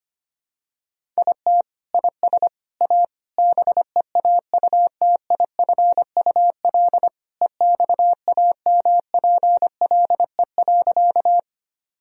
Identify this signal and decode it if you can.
Morse code, heard as a keyed tone, about 25 wpm, 700 Hz. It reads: IT IS A BEAUTIFUL EXAMPLE.
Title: Sentences from Top 1000 Words - Review Part 1 - 25wpm